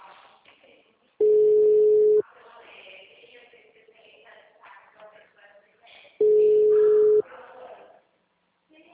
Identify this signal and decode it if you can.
Ringback tone (Europe, 425 Hz), with speech.